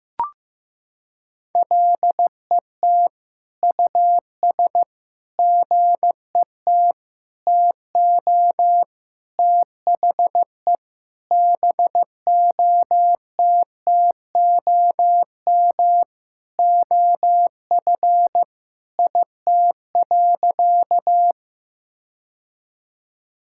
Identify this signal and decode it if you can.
Morse code, heard as a keyed tone, about 15 wpm, 700 Hz. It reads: LET US GET TO THE BOTTOM OF IT.
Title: Sentences from Top 1000 Words - Review Part 6 - 15wpm